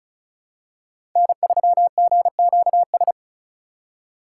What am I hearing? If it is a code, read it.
Morse code, heard as a keyed tone, about 36 wpm, 700 Hz. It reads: N3GQS
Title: USA Call Signs - 5 Characters - 2x Character Spacing - 35wpm